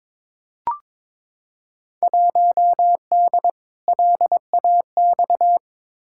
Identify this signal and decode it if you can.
Morse code, heard as a keyed tone, about 22 wpm, 700 Hz. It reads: 1D LAX
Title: Field Day Exchanges - 22wpm